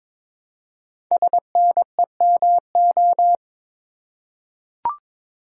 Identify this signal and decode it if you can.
Morse code, heard as a keyed tone, about 22 wpm, 700 Hz. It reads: SNEMO
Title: Code Groups - 5 Letters - 22wpm